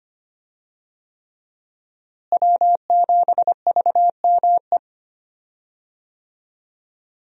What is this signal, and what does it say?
Morse code, heard as a keyed tone, about 25 wpm, 700 Hz. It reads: W7VME
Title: USA Call Signs - 5 Characters - Repeated 3x - 25wpm